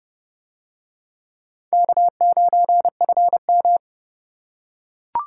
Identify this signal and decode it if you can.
Morse code, heard as a keyed tone, about 29 wpm, 700 Hz. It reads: K9FM